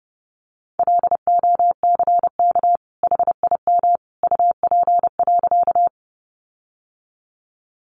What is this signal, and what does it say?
Morse code, heard as a keyed tone, about 30 wpm, 700 Hz. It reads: LOCK HIM UP.
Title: Sentences from Top 1300 Words 30wpm